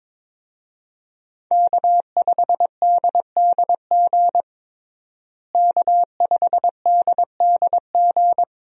Morse code, heard as a keyed tone, about 21 wpm, 700 Hz. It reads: K5DDG K5DDG
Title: USA Call Signs - 5 Characters - Repeated 2x - 22wpm